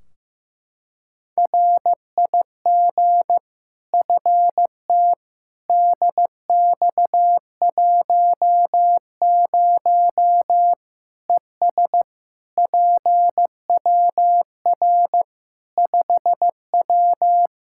Morse code, heard as a keyed tone, about 15 wpm, 700 Hz. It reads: RIG FT DX10 ES PWR 5W
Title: Practice QSO - Exchange 2 - Info - 15wpm